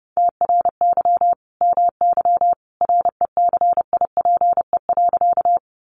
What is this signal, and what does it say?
Morse code, heard as a keyed tone, about 30 wpm, 700 Hz. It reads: TRY MY RECIPE.
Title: Sentences from Top 1200 Words 30wpm